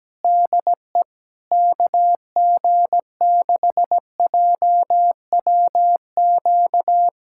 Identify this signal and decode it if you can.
Morse code, heard as a keyed tone, about 17 wpm, 700 Hz. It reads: DE KG6JWQ
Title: USA Callsign DE Callsign - 6 Characters - 17wpm